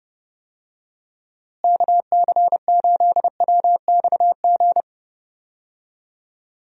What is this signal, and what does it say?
Morse code, heard as a keyed tone, about 30 wpm, 700 Hz. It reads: KC8WXG
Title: USA Call Signs - 6 Characters - Repeated 2x - 30wpm